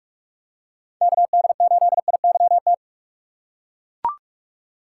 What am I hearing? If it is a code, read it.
Morse code, heard as a keyed tone, about 41 wpm, 700 Hz. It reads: KD8IYT